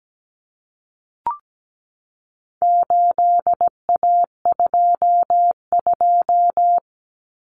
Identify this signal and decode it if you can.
Morse code, heard as a keyed tone, about 17 wpm, 700 Hz. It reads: 8A22
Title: Common Cut Numbers - 4-Digit 17wpm